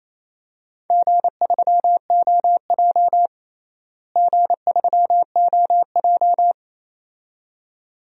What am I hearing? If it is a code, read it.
Morse code, heard as a keyed tone, about 27 wpm, 700 Hz. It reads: G3OJ G3OJ
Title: DXCC-100 Call Signs - 4 Characters - Repeated 2x - 28wpm